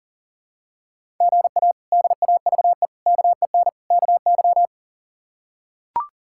Morse code, heard as a keyed tone, about 40 wpm, 700 Hz. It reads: GA DAVE KEN KY
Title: K1USN SST QSO Practice - 40wpm